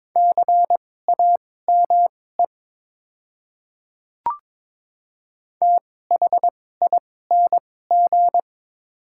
Morse code, heard as a keyed tone, about 22 wpm, spaced as for 16 wpm, 700 Hz. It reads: CAME THING
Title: 2x Character Spacing - Top 200 Words - 22wpm